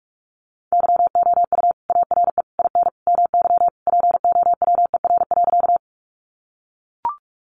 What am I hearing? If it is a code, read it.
Morse code, heard as a keyed tone, about 44 wpm, 700 Hz. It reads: YOU ARE IN MY POWER.
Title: Sentences from Top 500 Words 45wpm